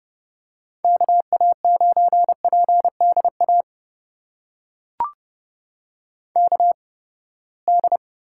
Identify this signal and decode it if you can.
Morse code, heard as a keyed tone, about 30 wpm, spaced as for 21 wpm, 700 Hz. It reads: KA9PDA KD
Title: USA Call Signs - 6 Characters - 8x Character Spacing - 30wpm